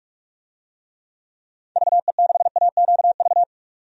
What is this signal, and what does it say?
Morse code, heard as a keyed tone, about 46 wpm, 700 Hz. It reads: VE6AQV
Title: DXCC-100 Call Signs - 6 Characters - 45wpm